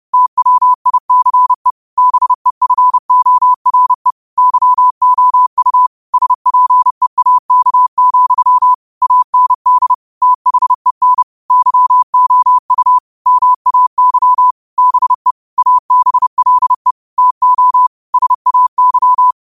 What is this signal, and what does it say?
Morse code, heard as a keyed tone, about 30 wpm, 1000 Hz. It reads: TWICE BEFORE YOU SPEAK, AND THEN YOU MAY BE ABLE TO SAY
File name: qotdfiles / 30wpm